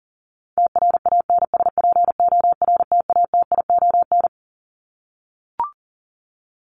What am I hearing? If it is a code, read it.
Morse code, heard as a keyed tone, about 40 wpm, 700 Hz. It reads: TRANSPORTATION
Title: Top 1501-1750 Words 40wpm